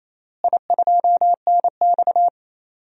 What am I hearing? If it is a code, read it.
Morse code, heard as a keyed tone, about 28 wpm, 700 Hz. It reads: I2NX
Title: DXCC-100 Call Signs - 4 Characters - Repeated 2x - 28wpm